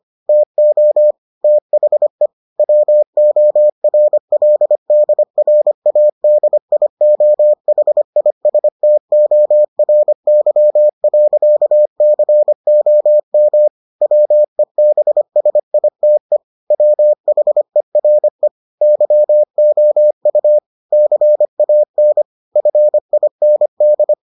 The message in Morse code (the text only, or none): TO THE WORLDRADIOHISTORY.COM WEBSITE WHERE YOU CAN FIND